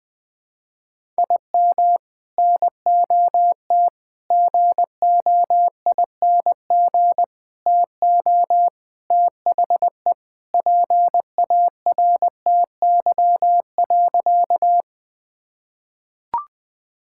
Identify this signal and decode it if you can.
Morse code, heard as a keyed tone, about 21 wpm, 700 Hz. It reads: IM NOT GOING TO THE PARTY.
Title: Sentences with 3-Word Phrases - Spoken Beginning - 20wpm